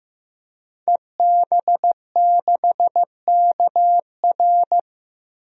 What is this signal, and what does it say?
Morse code, heard as a keyed tone, about 15 wpm, 700 Hz. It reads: EB6KR
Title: DXCC-100 Call Signs - 5 Characters - 15wpm